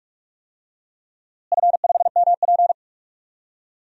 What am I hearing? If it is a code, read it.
Morse code, heard as a keyed tone, about 41 wpm, 700 Hz. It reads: F5MP
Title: DXCC-100 Call Signs - 4 Characters - Repeated 2x - 45wpm